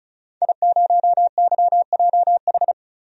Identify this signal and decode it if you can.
Morse code, heard as a keyed tone, about 36 wpm, 700 Hz. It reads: I0YJH
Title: DXCC-100 Call Signs - 5 Characters - 35wpm